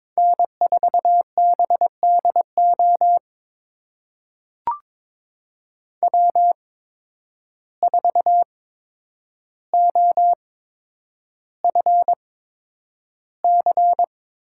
Morse code, heard as a keyed tone, about 22 wpm, spaced as for 9 wpm, 700 Hz. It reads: N4BDO W4OFC
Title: USA Call Signs - 5 Characters - 8x Character Spacing - 22wpm